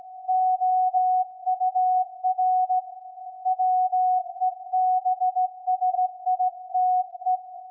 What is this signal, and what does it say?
Morse code, heard as a keyed tone, about 16 wpm, 740 Hz. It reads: OUR WEBSITE
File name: iz7slz / VLF / SAQ 24-12-2021